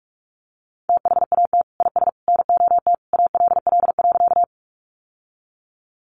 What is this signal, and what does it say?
Morse code, heard as a keyed tone, about 44 wpm, 700 Hz. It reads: THAT IS NOT ALL.
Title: Sentences from Top 600 Words 45wpm